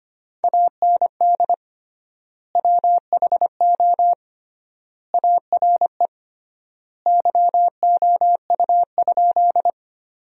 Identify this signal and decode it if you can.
Morse code, heard as a keyed tone, about 25 wpm, 700 Hz. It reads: AND WHO ARE YOU?
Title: 3x Word-Spacing - Sentences from Top 100 Words - 25wpm